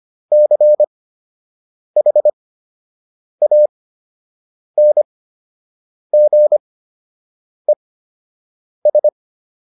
Morse code, heard as a keyed tone, about 25 wpm, spaced as for 7 wpm, 600 Hz. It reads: CHANGES